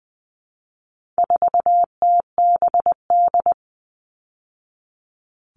Morse code, heard as a keyed tone, about 20 wpm, 700 Hz. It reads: 4TBD
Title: All Cut Numbers - 4-Digit 20wpm